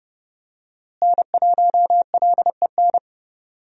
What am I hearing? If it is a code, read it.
Morse code, heard as a keyed tone, about 30 wpm, 700 Hz. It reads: N1LEN
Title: USA Call Signs - 5 Characters - Repeated 3x - 30wpm